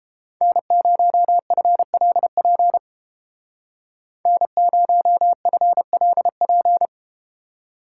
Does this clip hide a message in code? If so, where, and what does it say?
Yes: Morse code from 0.4 to 6.9 s, heard as a keyed tone, about 31 wpm, 700 Hz. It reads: N0FLP N0FLP